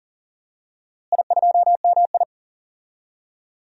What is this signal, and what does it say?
Morse code, heard as a keyed tone, about 40 wpm, 700 Hz. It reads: I2MI
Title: DXCC-100 Call Signs - 4 Characters - Repeated 2x - 40wpm